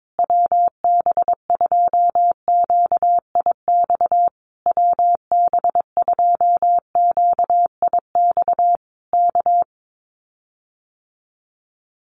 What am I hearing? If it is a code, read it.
Morse code, heard as a keyed tone, about 22 wpm, 700 Hz. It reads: WB2QIX WB2QIX K
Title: Common QSO Phrases Encore 22wpm